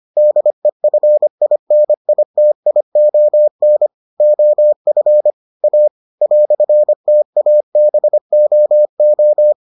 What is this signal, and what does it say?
Morse code, heard as a keyed tone, about 25 wpm, 600 Hz. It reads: DEFINITION OF A "TABOO